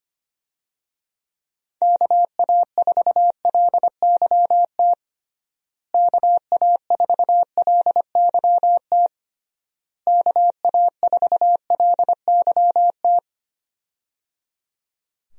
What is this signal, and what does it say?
Morse code, heard as a keyed tone, about 25 wpm, 700 Hz. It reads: KA4LYT KA4LYT KA4LYT
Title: USA Call Signs - 6 Characters - Repeated 3x - 25wpm